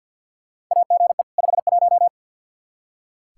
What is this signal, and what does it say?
Morse code, heard as a keyed tone, about 45 wpm, 700 Hz. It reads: AGE 51
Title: Practice QSO - Exchange 3 - More Info - 50wpm